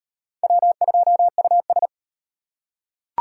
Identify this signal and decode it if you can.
Morse code, heard as a keyed tone, about 37 wpm, 700 Hz. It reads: W2US